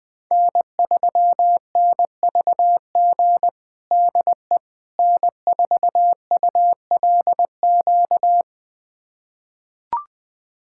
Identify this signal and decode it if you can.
Morse code, heard as a keyed tone, about 20 wpm, 700 Hz. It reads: N3NVG DE N4ULQ
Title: USA Callsign DE Callsign - 5 Characters - 20wpm